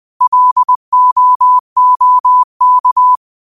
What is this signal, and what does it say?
Morse code, heard as a keyed tone, about 20 wpm, 1000 Hz. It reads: LOOK